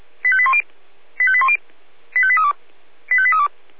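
Five-tone selective call (ZVEI), heard as quick runs of five tones, tones of about 70 ms. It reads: 85810 85810 85832 85832
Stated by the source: ZVEI five-tone selective call received on an SDR.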